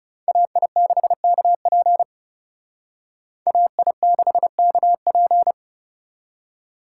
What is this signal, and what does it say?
Morse code, heard as a keyed tone, about 36 wpm, 700 Hz. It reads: AI6KP AI6KP
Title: Speed Racing - USA 2x2 Call Signs 30wpm